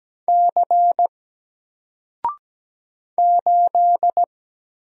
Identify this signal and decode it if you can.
Morse code, heard as a keyed tone, about 17 wpm, 700 Hz. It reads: C 8
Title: Single Character (Letter:Number:Punctuation:Prosign) Alt 17wpm